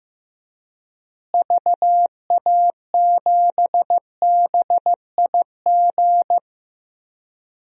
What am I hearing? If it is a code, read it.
Morse code, heard as a keyed tone, about 16 wpm, 700 Hz. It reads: VA7BIG
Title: DXCC-100 Call Signs - 6 Characters - Repeated 3x - 15wpm